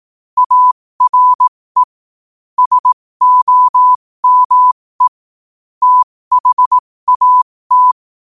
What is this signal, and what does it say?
Morse code, heard as a keyed tone, about 18 wpm, spaced as for 15 wpm, 1000 Hz. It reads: ARE SOME THAT
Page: New quotes every day in morse code at 15 Words per minute.